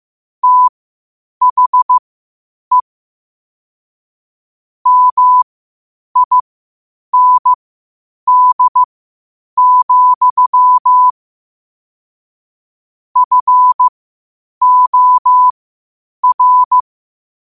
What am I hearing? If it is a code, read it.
Morse code, heard as a keyed tone, about 15 wpm, spaced as for 8 wpm, 1000 Hz. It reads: THE MIND, FOR
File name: qotdfiles / 8wpm